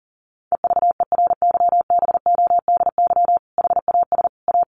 Morse code, heard as a keyed tone, about 40 wpm, 700 Hz. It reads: EVERYBODY HAS A
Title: Sentences from Top 1400 Words 40wpm